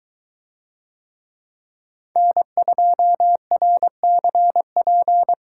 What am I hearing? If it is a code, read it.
Morse code, heard as a keyed tone, about 23 wpm, 700 Hz. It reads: N2RCP